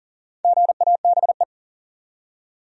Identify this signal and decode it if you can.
Morse code, heard as a keyed tone, about 39 wpm, 700 Hz. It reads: GABE
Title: USA Common Names - Max 5 Letters - 40wpm